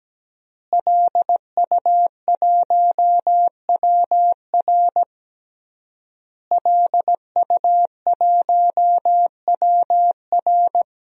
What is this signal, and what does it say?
Morse code, heard as a keyed tone, about 17 wpm, 700 Hz. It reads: LU1WR LU1WR
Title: DXCC-100 Call Signs - 5 Characters - Repeated 3x - 17wpm